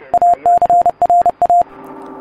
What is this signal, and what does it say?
Morse code, heard as a keyed tone, about 30 wpm, 700 Hz. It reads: ACRA